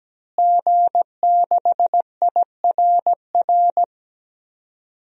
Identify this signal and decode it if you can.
Morse code, heard as a keyed tone, about 17 wpm, 700 Hz. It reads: G6IRR
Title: DXCC-100 Call Signs - 5 Characters - Repeated 3x - 17wpm